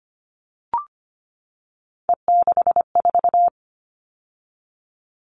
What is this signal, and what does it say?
Morse code, heard as a keyed tone, about 25 wpm, 700 Hz. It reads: E64
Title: All Cut Numbers - 3-Digit 25wpm